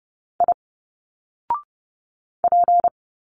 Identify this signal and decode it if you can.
Morse code, heard as a keyed tone, about 30 wpm, 700 Hz. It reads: I P
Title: Single Letters 30wpm